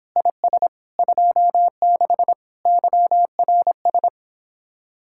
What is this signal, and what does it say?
Morse code, heard as a keyed tone, about 29 wpm, 700 Hz. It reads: IS 26 YRS